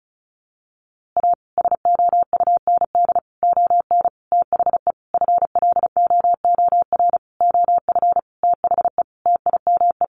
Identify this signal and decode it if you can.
Morse code, heard as a keyed tone, about 34 wpm, 700 Hz. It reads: A SOUND ON THE FLOOR OF THE TIME
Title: A Sound Of Thunder 35wpm